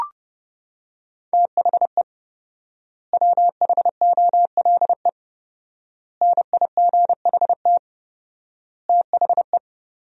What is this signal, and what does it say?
Morse code, heard as a keyed tone, about 32 wpm, 700 Hz. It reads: THE WHOLE NIGHT THE
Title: 4x Word-Spacing - Sentences from Top 500 Words - 30wpm